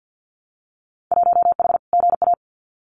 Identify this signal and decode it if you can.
Morse code, heard as a keyed tone, about 50 wpm, 700 Hz. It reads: 1H GA
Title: Winter Field Day Exchanges 50wpm